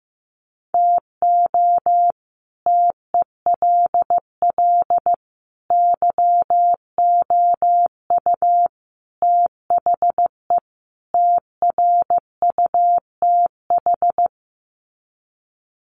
Morse code, heard as a keyed tone, about 15 wpm, 700 Hz. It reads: TO TELL YOU THE TRUTH
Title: Sets of 5 Words 15wpm